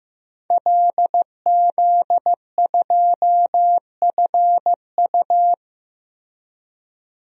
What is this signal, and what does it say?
Morse code, heard as a keyed tone, about 15 wpm, 700 Hz. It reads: LZ2FU
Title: DXCC-100 Call Signs - 5 Characters - Repeated 2x - 15wpm